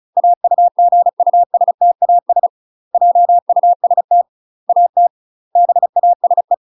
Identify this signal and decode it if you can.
Morse code, heard as a keyed tone, about 35 wpm, 700 Hz. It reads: AUGUSTAS JUST AT BASE